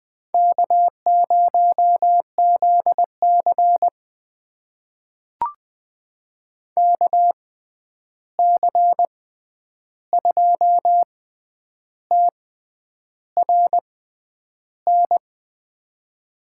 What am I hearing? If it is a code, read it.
Morse code, heard as a keyed tone, about 20 wpm, spaced as for 9 wpm, 700 Hz. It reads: K0ZC KC2TRN
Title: USA Call Signs - 6x Character Spacing - 20wpm